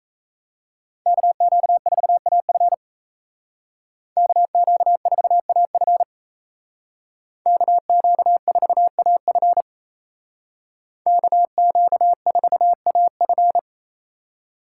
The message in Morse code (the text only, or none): KQ4AF KQ4AF KQ4AF KQ4AF